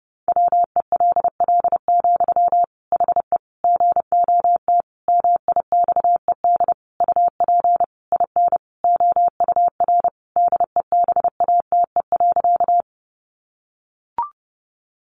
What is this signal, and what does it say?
Morse code, heard as a keyed tone, about 30 wpm, 700 Hz. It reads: WELL, HE GOT MIXED UP IN OUR DEBATE.
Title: Sentences from Top 1500 Words 30wpm